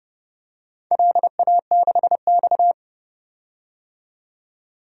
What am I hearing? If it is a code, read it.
Morse code, heard as a keyed tone, about 32 wpm, 700 Hz. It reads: LA6X
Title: DXCC-100 Call Signs - 4 Characters - 30wpm